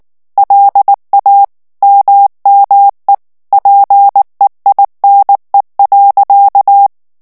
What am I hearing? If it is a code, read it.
Morse code, heard as a keyed tone, about 19 wpm, 800 Hz. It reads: LA MME PEINE.
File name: CW / FICHIERS-MP3 / FRANCAIS TEXTE CLAIR 2 VITESSE 18 MOTS MINUTES